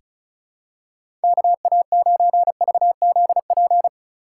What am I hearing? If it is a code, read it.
Morse code, heard as a keyed tone, about 36 wpm, 700 Hz. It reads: KA9VZP
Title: USA Call Signs - 4x Character Spacing - 35wpm